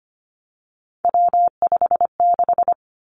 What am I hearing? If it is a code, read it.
Morse code, heard as a keyed tone, about 25 wpm, 700 Hz. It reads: W56
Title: Three Challenging Letter-Number Combinations 25wpm